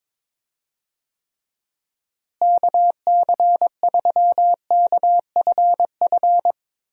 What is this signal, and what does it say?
Morse code, heard as a keyed tone, about 22 wpm, 700 Hz. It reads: KC3KFF
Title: USA Call Signs - 6 Characters - Repeated 2x - 22wpm